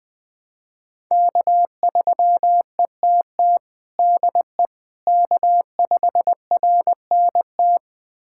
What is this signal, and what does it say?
Morse code, heard as a keyed tone, about 20 wpm, 700 Hz. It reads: K3ETT DE K5RNT
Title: USA Callsign DE Callsign - 5 Characters - 20wpm